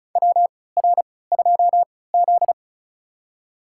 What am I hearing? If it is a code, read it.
Morse code, heard as a keyed tone, about 36 wpm, spaced as for 20 wpm, 700 Hz. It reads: WR2Z
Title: USA Call Signs - 3x Character Spacing - 35wpm